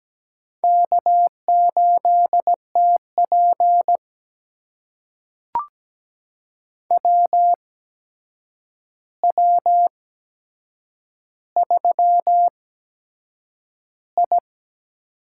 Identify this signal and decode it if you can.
Morse code, heard as a keyed tone, about 17 wpm, spaced as for 7 wpm, 700 Hz. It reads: K8TP WW3I
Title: USA Call Signs - 4 Characters - 8x Character Spacing - 17wpm